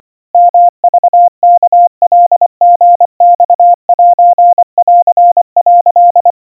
Morse code (Japanese Wabun code), heard as a keyed tone, about 25 wpm, 700 Hz. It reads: ヨクワカリマセン」
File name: 90 cpm 700 Hz Crispy